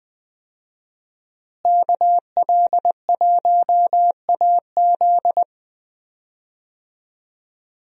Morse code, heard as a keyed tone, about 20 wpm, 700 Hz. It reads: KL1AZ
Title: USA Call Signs - 5 Characters - Repeated 3x - 20wpm